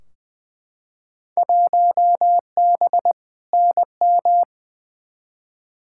Morse code, heard as a keyed tone, about 20 wpm, 700 Hz. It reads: 1B NM
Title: Field Day Exchanges Encore - 20wpm